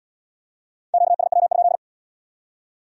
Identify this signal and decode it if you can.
Morse code, heard as a keyed tone, about 75 wpm, 700 Hz. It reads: QSW?